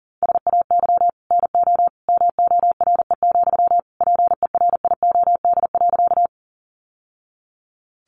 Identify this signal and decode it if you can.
Morse code, heard as a keyed tone, about 40 wpm, 690 Hz. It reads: SAY NO MORE, PERIOD.
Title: Sentences from Top 800 Words 40wpm